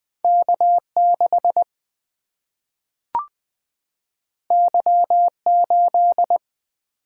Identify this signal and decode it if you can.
Morse code, heard as a keyed tone, about 20 wpm, 700 Hz. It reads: K6 Y8
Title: Single-Letter Single-Number 20wpm